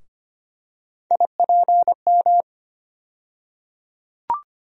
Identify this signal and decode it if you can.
Morse code, heard as a keyed tone, about 25 wpm, 700 Hz. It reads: IPM
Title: USA Call Sign Postfixes - 3 Letters 25wpm